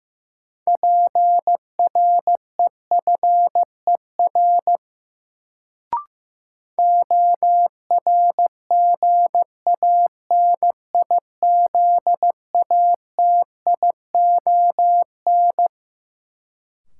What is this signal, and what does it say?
Morse code, heard as a keyed tone, about 15 wpm, 700 Hz. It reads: PREFER ORGANIZATION
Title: Sentences from Top 1100 Words - New Word Drill Alt - 15wpm